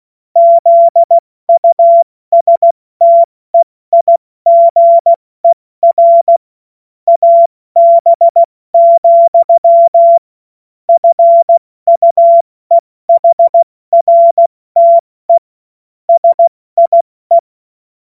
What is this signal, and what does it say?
Morse code, heard as a keyed tone, about 16 wpm, 680 Hz. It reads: ZUSTEIGER AB, FUEHRTE SIE
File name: Morsekurs / Witze / Witz 16 Januar 2024 70 F80